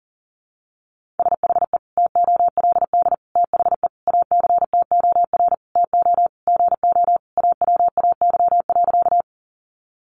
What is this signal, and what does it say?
Morse code, heard as a keyed tone, about 37 wpm, 700 Hz. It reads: SHE TOLD THE ACTOR TO GO AWAY.
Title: Sentences from Top 1500 Words 40wpm